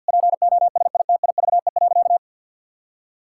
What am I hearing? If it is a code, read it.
Morse code, heard as a keyed tone, about 52 wpm, 700 Hz. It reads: POSITIVE.
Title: Sentences from Top 1000 Words - Review Part 5 - 50wpm